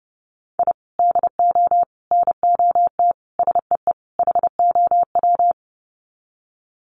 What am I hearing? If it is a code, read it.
Morse code, heard as a keyed tone, about 30 wpm, 700 Hz. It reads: I DO NOT SEE HOW
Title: Sets of 5 Words 30wpm